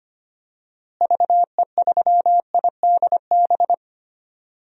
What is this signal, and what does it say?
Morse code, heard as a keyed tone, about 25 wpm, 700 Hz. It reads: VE3IDB
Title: DXCC-100 Call Signs - 6 Characters - Repeated 2x - 25wpm